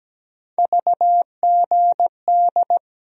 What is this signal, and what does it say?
Morse code, heard as a keyed tone, about 17 wpm, 700 Hz. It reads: VGD
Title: USA Call Sign Postfixes - 3 Letters 17wpm